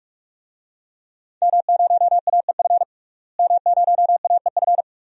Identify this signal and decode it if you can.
Morse code, heard as a keyed tone, about 46 wpm, 700 Hz. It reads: M0AEF M0AEF
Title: DXCC-100 Call Signs - 5 Characters - Repeated 2x - 45wpm